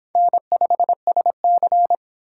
Morse code, heard as a keyed tone, about 24 wpm, 700 Hz. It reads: N5SC